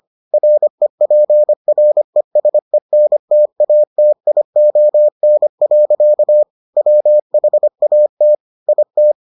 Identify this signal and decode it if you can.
Morse code, heard as a keyed tone, about 25 wpm, 600 Hz. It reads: REPRESENTATION. WHAT IT